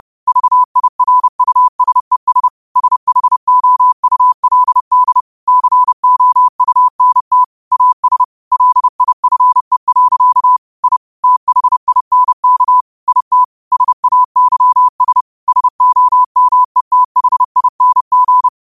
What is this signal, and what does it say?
Morse code, heard as a keyed tone, about 30 wpm, 1000 Hz. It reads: VIRUSES SHOULD COUNT AS LIFE. I THINK IT SAYS SOMETHING
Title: 30 WPM morse code quotes for Fri, 01 Mar 2024